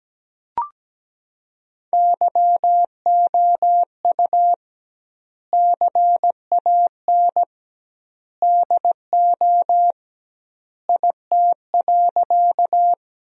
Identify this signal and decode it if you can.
Morse code, heard as a keyed tone, about 17 wpm, 700 Hz. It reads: YOU CAN DO IT.
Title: 2x Word-Spacing - Sentences from Top 100 Words - 17wpm